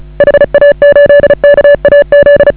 Morse code, heard as a keyed tone, about 35 wpm, 560 Hz. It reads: HA8KAZ